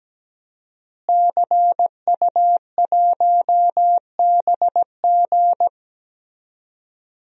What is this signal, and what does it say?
Morse code, heard as a keyed tone, about 17 wpm, 700 Hz. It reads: CU1BG
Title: DXCC-100 Call Signs - 5 Characters - Repeated 2x - 17wpm